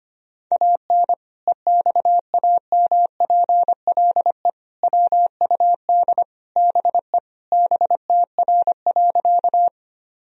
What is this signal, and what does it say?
Morse code, heard as a keyed tone, about 25 wpm, 700 Hz. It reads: AN EXAMPLE WUD BE BTR.
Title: Sentences from Top 400 Words and Abbreviations 25wpm